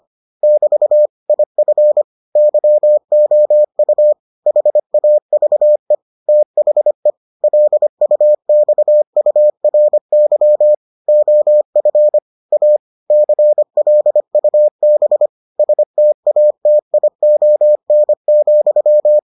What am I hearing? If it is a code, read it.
Morse code, heard as a keyed tone, about 25 wpm, 600 Hz. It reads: <BT> IF YOU HAVE THE LUXURY OF A CLUB STATION,